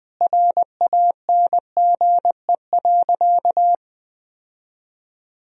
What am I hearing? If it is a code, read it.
Morse code, heard as a keyed tone, about 21 wpm, 700 Hz. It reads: RANGE.
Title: Sentences from Top 1000 Words - Review Part 4 - 20wpm